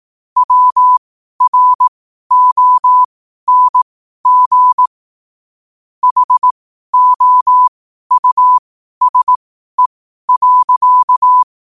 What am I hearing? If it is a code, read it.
Morse code, heard as a keyed tone, about 18 wpm, spaced as for 12 wpm, 1000 Hz. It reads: WRONG HOUSE.
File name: qotdfiles / 12wpm